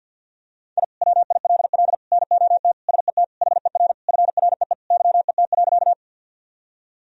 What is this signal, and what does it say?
Morse code, heard as a keyed tone, about 52 wpm, 700 Hz. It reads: I WILL NOT SET HER FREE YET.
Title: Sentences from Top 1000 Words - Review Part 1 - 50wpm